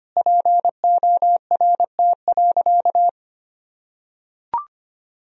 Morse code, heard as a keyed tone, about 25 wpm, 700 Hz. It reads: PORT.
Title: Sentences from Top 1000 Words - Review Part 1 - 25wpm